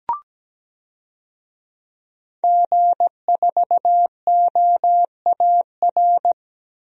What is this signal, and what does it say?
Morse code, heard as a keyed tone, about 17 wpm, 700 Hz. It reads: G4OAR
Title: DXCC-100 Call Signs - 5 Characters - 17wpm